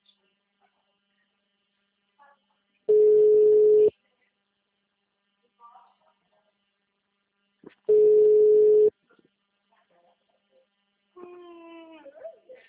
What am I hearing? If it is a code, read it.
Ringback tone (Europe, 425 Hz), with speech.